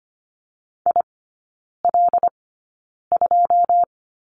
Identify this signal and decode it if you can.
Morse code, heard as a keyed tone, about 25 wpm, spaced as for 9 wpm, 700 Hz. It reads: IL2
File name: Lesson 15e - Number 5 - Three Characters 25wpm